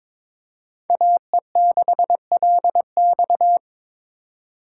Morse code, heard as a keyed tone, about 26 wpm, 700 Hz. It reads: AE6LX
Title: Speed Racing - USA 2x2 Call Signs 22wpm